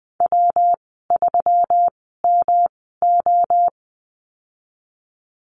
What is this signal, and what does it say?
Morse code, heard as a keyed tone, about 20 wpm, spaced as for 14 wpm, 700 Hz. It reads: W3MO
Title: 161 Lesson 26g - Letter g - Call Signs 20wpm